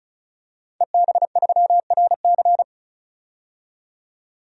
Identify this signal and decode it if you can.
Morse code, heard as a keyed tone, about 36 wpm, 700 Hz. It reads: EB3RC
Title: DXCC-100 Call Signs - 5 Characters - 35wpm